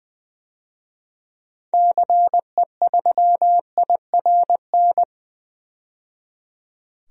Morse code, heard as a keyed tone, about 20 wpm, 700 Hz. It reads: CE3IRN
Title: DXCC-100 Call Signs - 6 Characters - 20wpm